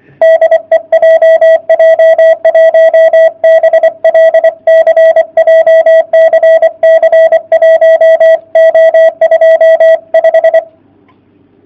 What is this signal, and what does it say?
Morse code, heard as a keyed tone, about 24 wpm, 660 Hz. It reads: DE JJ1BLC JCC 1O25
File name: jh8cbh / 2019cw / sp31- w